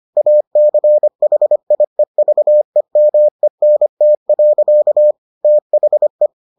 Morse code, heard as a keyed tone, about 25 wpm, 600 Hz. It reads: ACHIEVEMENT. THE